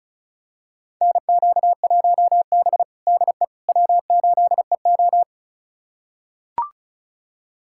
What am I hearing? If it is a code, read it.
Morse code, heard as a keyed tone, about 36 wpm, 700 Hz. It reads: NQ1B DE W8EO
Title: USA Callsign DE Callsign - 4 Characters - 35wpm